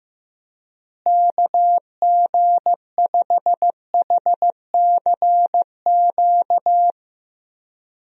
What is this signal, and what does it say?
Morse code, heard as a keyed tone, about 15 wpm, 700 Hz. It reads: KG5HCQ
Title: USA Call Signs - 6 Characters - Repeated 2x - 15wpm